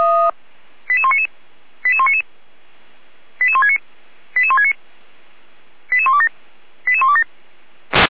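Five-tone selective call (ZVEI), heard as quick runs of five tones, tones of about 70 ms. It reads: 88199 88199 88169 88169 88127 88127
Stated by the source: ZVEI five-tone selective call 88124 received by SDR.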